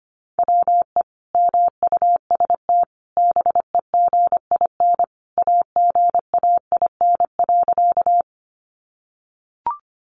Morse code, heard as a keyed tone, about 25 wpm, 700 Hz. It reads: WE MUST BEGIN AGAIN.
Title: Sentences from Top 300 Words 25wpm